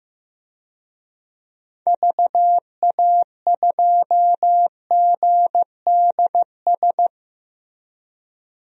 Morse code, heard as a keyed tone, about 15 wpm, 700 Hz. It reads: VA2GDS
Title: DXCC-100 Call Signs - 6 Characters - Repeated 2x - 15wpm